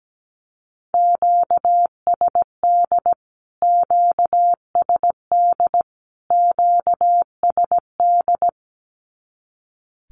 Morse code, heard as a keyed tone, about 17 wpm, 700 Hz. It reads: QSD QSD QSD
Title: Q-Codes Repeated 3x 17wpm